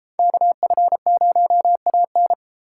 Morse code, heard as a keyed tone, about 31 wpm, 700 Hz. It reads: KF0AN